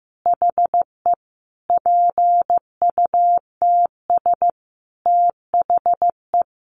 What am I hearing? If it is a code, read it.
Morse code, heard as a keyed tone, about 15 wpm, 700 Hz. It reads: HE PUTS THE
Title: Additional Easy Phrases 15wpm